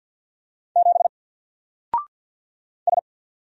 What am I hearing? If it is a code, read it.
Morse code, heard as a keyed tone, about 48 wpm, 700 Hz. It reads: 7 S
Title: Single Character (Letter:Number:Punctuation:Prosign) 50wpm